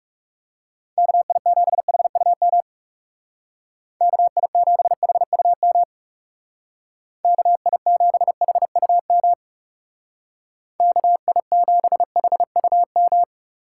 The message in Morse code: KI7HUM KI7HUM KI7HUM KI7HUM

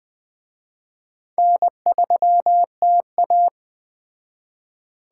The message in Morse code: N3TA

